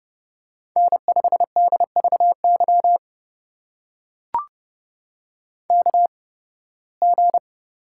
Morse code, heard as a keyed tone, about 30 wpm, spaced as for 20 wpm, 700 Hz. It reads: N5DVY KG
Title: USA Call Signs - 5 Characters - 8x Character Spacing - 30wpm